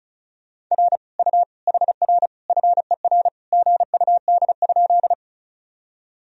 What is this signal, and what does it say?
Morse code, heard as a keyed tone, about 36 wpm, 700 Hz. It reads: R U HR FER GUD?
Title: Sentences from Top 200 Words and Abbreviations 35wpm